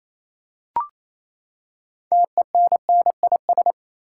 Morse code, heard as a keyed tone, about 28 wpm, 700 Hz. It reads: TENNIS
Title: Six Letter Words - 28wpm